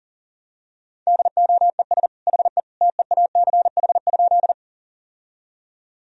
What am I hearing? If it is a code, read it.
Morse code, heard as a keyed tone, about 40 wpm, 700 Hz. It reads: DOES HE TEACH?
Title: Sentences from Top 1000 Words - Review Part 2 - 40wpm